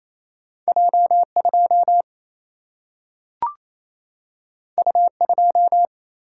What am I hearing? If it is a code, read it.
Morse code, heard as a keyed tone, about 28 wpm, 700 Hz. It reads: J 2 U 2
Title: Single-Letter Single-Number 28wpm